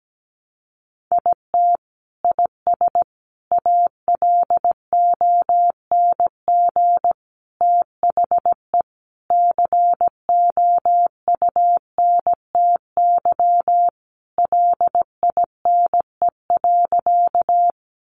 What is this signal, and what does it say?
Morse code, heard as a keyed tone, about 17 wpm, 700 Hz. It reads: IT IS ALONG THE COUNTY LINE.
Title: Sentences from Top 1200 Words 17wpm